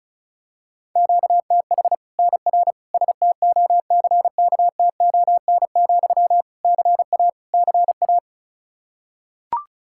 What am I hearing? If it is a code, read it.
Morse code, heard as a keyed tone, about 34 wpm, 700 Hz. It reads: QTH NR STOCKTON, CA CA
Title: Practice QSO - Exchange 1 - Beginning - 35wpm